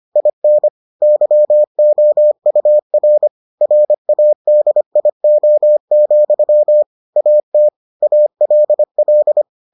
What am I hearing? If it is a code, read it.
Morse code, heard as a keyed tone, about 25 wpm, 600 Hz. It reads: IN YOUR RADIO, AT ALL